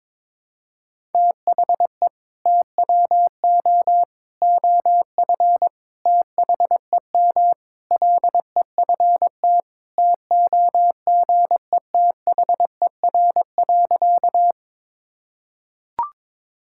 Morse code, heard as a keyed tone, about 22 wpm, 700 Hz. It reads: THE TWO OF THEM LEFT TOGETHER.
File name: practice / 22wpm / Sentences with 4-Word Phrases - 22wpm